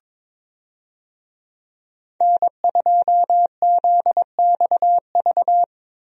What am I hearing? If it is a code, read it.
Morse code, heard as a keyed tone, about 22 wpm, 700 Hz. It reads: N2ZXV
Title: USA Call Signs - 5 Characters - Repeated 3x - 22wpm